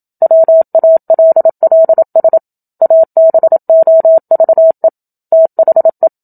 Morse code, heard as a keyed tone, about 27 wpm, 650 Hz. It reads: WALLS ABOVE THE
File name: Book Two / 01-Chapter 1 - 27 WPM